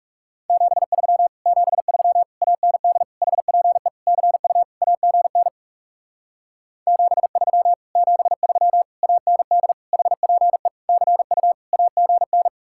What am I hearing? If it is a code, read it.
Morse code, heard as a keyed tone, about 40 wpm, 700 Hz. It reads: 73 73 AND HPE CU AGN 73 73 AND HPE CU AGN